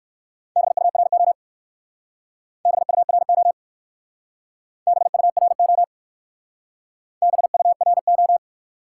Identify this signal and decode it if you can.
Morse code, heard as a keyed tone, about 56 wpm, 700 Hz. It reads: BURO BURO BURO BURO
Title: Speed-Racing Top 90 QSO Elements 45wpm